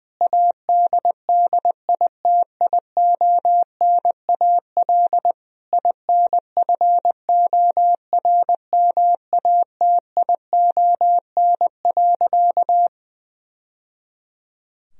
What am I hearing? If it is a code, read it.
Morse code, heard as a keyed tone, about 20 wpm, 700 Hz. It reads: ADDITIONAL INFORMATION.
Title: Sentences from Top 1000 Words - Review Part 4 - 20wpm